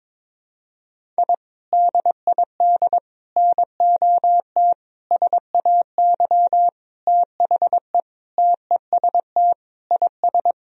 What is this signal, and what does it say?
Morse code, heard as a keyed tone, about 22 wpm, 700 Hz. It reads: I DID NOT SAY THE TEST IS
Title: Sentences from Top 1000 Words - Review Part 3 - 22wpm